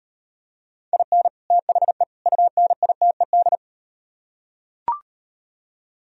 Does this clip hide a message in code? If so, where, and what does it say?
Yes: Morse code from 0.9 to 3.6 s, heard as a keyed tone, about 38 wpm, 700 Hz. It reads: IN THE UNITED